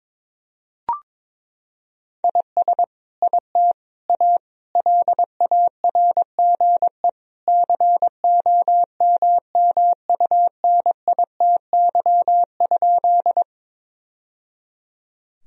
Morse code, heard as a keyed tone, about 22 wpm, 700 Hz. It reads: IS IT A LARGE COMMUNITY?
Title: Sentences from Top 1000 Words - Review Part 4 - 22wpm